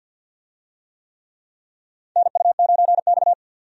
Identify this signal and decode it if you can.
Morse code, heard as a keyed tone, about 51 wpm, 700 Hz. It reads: NU9X
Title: USA Call Signs - 4 Characters - 50wpm